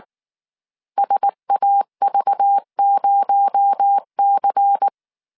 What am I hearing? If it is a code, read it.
Morse code, heard as a keyed tone, about 19 wpm, 800 Hz. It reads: SAV0C
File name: CW / DET / 1 6 1500C S C 18wpm 2356